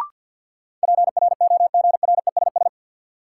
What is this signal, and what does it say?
Morse code, heard as a keyed tone, about 52 wpm, 700 Hz. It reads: PROGRESS
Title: Sentences from Top 1400 Words - New Word Drill - 50wpm